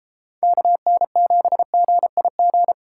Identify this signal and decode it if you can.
Morse code, heard as a keyed tone, about 33 wpm, 700 Hz. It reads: KN7GIG